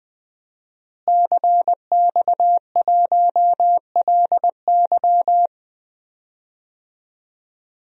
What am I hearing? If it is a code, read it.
Morse code, heard as a keyed tone, about 20 wpm, 700 Hz. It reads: CX1LY
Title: DXCC-100 Call Signs - 5 Characters - Repeated 2x - 20wpm